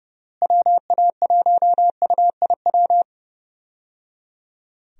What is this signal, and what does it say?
Morse code, heard as a keyed tone, about 30 wpm, 700 Hz. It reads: WA1UIW
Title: USA Call Signs - 6 Characters - Repeated 3x - 30wpm